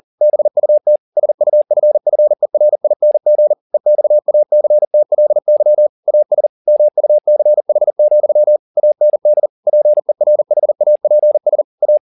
Morse code, heard as a keyed tone, about 40 wpm, 600 Hz. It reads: BUT SUFFERING EXACTLY AS MUCH, AND PERHAPS A